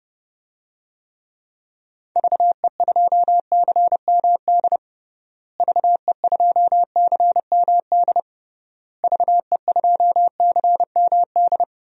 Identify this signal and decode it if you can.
Morse code, heard as a keyed tone, about 30 wpm, 700 Hz. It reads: VE2CMD VE2CMD VE2CMD
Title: DXCC-100 Call Signs - 6 Characters - Repeated 3x - 30wpm